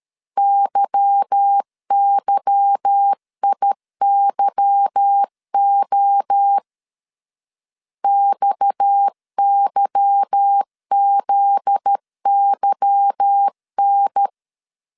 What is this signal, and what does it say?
Morse code, heard as a keyed tone, about 13 wpm, 800 Hz. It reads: YYIYO XYZYN
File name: LEZIONI / 8 YJ45 / 8 Y 12wpm 2228